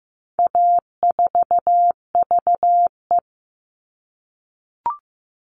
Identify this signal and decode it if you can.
Morse code, heard as a keyed tone, about 15 wpm, 700 Hz. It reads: A4VE
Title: All Cut Numbers - 4-Digit 15wpm